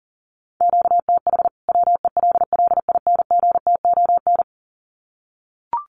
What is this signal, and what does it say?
Morse code, heard as a keyed tone, about 40 wpm, 700 Hz. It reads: QTH WELLINGTON
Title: Common QSO Phrases 40wpm